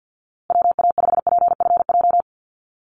Morse code, heard as a keyed tone, about 50 wpm, 700 Hz. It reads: WA5PFJ
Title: Call Signs Repeated 2x - 50wpm